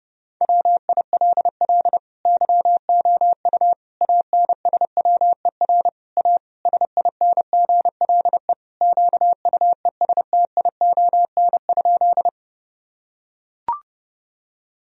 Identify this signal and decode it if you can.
Morse code, heard as a keyed tone, about 30 wpm, 700 Hz. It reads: WILL YOU ANSWER A SINGLE QUESTION?
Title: Sentences from Top 1000 Words - Review Part 4 - 30wpm